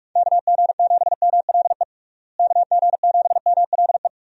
Morse code, heard as a keyed tone, about 46 wpm, 700 Hz. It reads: KG7MLE KG7MLE
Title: USA Call Signs - 6 Characters - Repeated 2x - 45wpm